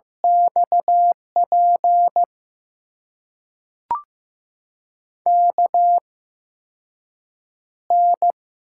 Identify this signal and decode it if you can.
Morse code, heard as a keyed tone, about 15 wpm, spaced as for 6 wpm, 700 Hz. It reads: XP KN
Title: USA Call Signs - 6 Characters - 8x Character Spacing - 15wpm